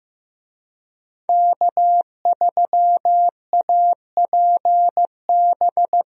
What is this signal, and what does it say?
Morse code, heard as a keyed tone, about 15 wpm, 700 Hz. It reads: K3APB
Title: USA Call Signs - 5 Characters - Repeated 2x - 15wpm